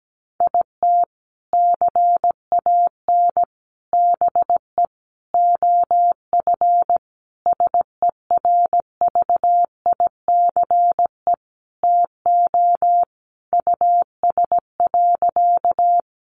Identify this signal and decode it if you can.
Morse code, heard as a keyed tone, about 17 wpm, 700 Hz. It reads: IT CAN BE OF SERVICE TO US.
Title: Sentences from Top 1200 Words 17wpm